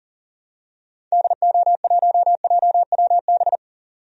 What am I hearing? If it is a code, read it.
Morse code, heard as a keyed tone, about 38 wpm, 700 Hz. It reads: DO1JWB
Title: DXCC-100 Call Signs - 6 Characters - 40wpm